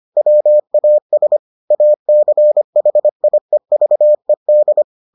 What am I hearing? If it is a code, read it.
Morse code, heard as a keyed tone, about 25 wpm, 600 Hz. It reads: WAS ACHIEVED